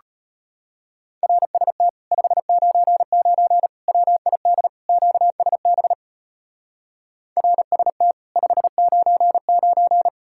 Morse code, heard as a keyed tone, about 32 wpm, 700 Hz. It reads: RST 599 WID QSB RST 599